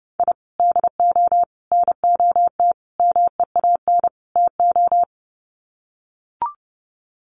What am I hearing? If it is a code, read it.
Morse code, heard as a keyed tone, about 30 wpm, 700 Hz. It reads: I DO NOT MEAN TO
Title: Sets of 5 Words 30wpm